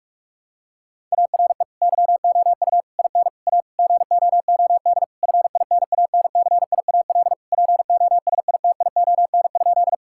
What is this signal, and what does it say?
Morse code, heard as a keyed tone, about 41 wpm, 700 Hz. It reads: ARE YOU IN A GOOD FINANCIAL POSITION?
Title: Sentences from Top 1000 Words - Review Part 5 - 45wpm